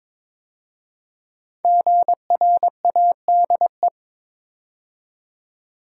Morse code, heard as a keyed tone, about 22 wpm, 700 Hz. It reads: GRADE
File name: 5 Letter Words - First 3 Letters Given - 22wpm